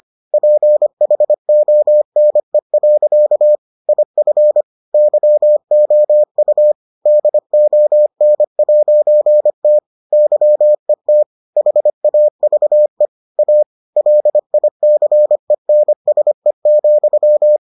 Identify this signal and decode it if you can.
Morse code, heard as a keyed tone, about 25 wpm, 600 Hz. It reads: PHONE. IF YOU DON'T YET HAVE A LICENSE,